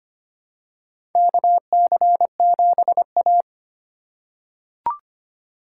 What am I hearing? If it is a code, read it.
Morse code, heard as a keyed tone, about 25 wpm, 700 Hz. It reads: KC7A
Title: USA Call Signs - 4 Characters - 3x Character Spacing - 25wpm